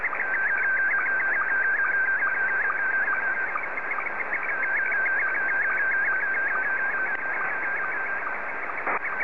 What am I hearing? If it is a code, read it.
SSTV (slow-scan television), Scottie 1.